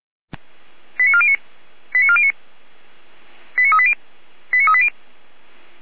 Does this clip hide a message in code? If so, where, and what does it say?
Yes: five-tone selective call (ZVEI) from 1.0 to 4.9 s, heard as quick runs of five tones, tones of about 70 ms. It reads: 89409 89409 89380 89380